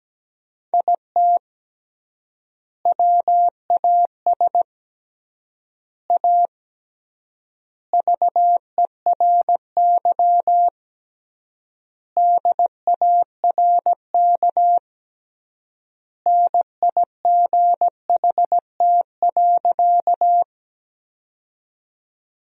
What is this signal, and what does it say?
Morse code, heard as a keyed tone, about 17 wpm, 700 Hz. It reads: IT WAS A VERY DARK NIGHT.
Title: 3x Word-Spacing - Sentences from Top 500 Words - 17wpm